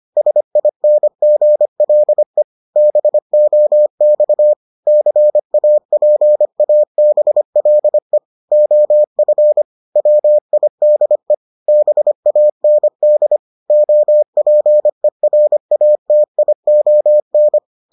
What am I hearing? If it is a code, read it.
Morse code, heard as a keyed tone, about 25 wpm, 600 Hz. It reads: SINGLE BOX CAPABLE OF WIDE BAND OPERATION